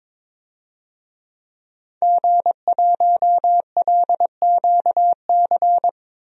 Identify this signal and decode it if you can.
Morse code, heard as a keyed tone, about 22 wpm, 700 Hz. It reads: G1LQC
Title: DXCC-100 Call Signs - 5 Characters - 22wpm